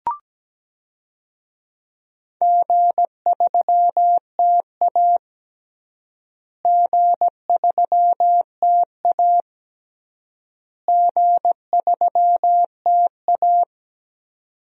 Morse code, heard as a keyed tone, about 17 wpm, 700 Hz. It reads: G3TA G3TA G3TA
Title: DXCC-100 Call Signs - 4 Characters - Repeated 3x - 17wpm